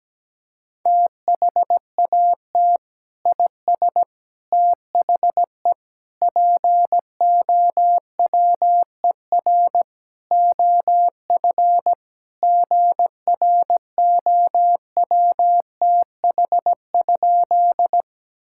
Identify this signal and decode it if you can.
Morse code, heard as a keyed tone, about 17 wpm, 700 Hz. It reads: THAT IS THE POWER OF GROWTH?
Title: Sentences from Top 1000 Words - Review Part 5 - 17wpm